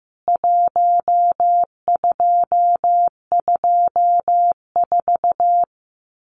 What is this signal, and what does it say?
Morse code, heard as a keyed tone, about 15 wpm, 700 Hz. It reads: 1224
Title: CWT Exchange 15wpm